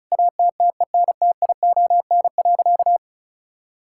Morse code, heard as a keyed tone, about 34 wpm, 700 Hz. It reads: ATTENTION.
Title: Sentences from Top 1000 Words - Review Part 6 - 35wpm